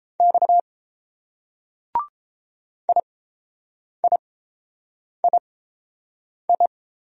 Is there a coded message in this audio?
Yes: Morse code from 0.2 to 6.7 s, heard as a keyed tone, about 30 wpm, spaced as for 7 wpm, 700 Hz. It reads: X IIII